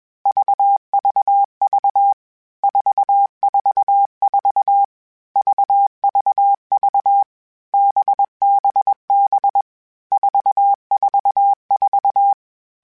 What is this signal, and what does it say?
Morse code, heard as a keyed tone, about 21 wpm, 800 Hz. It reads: VVV 444 VVV BBB 444
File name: Basic Sets / Basic Letter Groups Collective7 / VB46@20